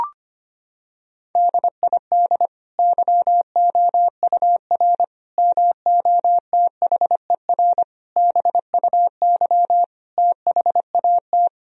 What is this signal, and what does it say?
Morse code, heard as a keyed tone, about 25 wpm, 700 Hz. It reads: DID YOUR MOTHER BUY THAT
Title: Sentences from Top 1000 Words - Review Part 3 - 25wpm